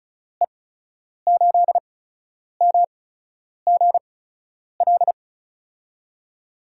Morse code, heard as a keyed tone, about 36 wpm, spaced as for 10 wpm, 700 Hz. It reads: E8MGL
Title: USA Call Signs - 6 Characters - 8x Character Spacing - 35wpm